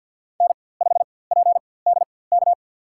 Morse code, heard as a keyed tone, about 52 wpm, spaced as for 24 wpm, 700 Hz. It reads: N5PDK
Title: USA Call Signs - 5 Characters - 4x Character Spacing - 50wpm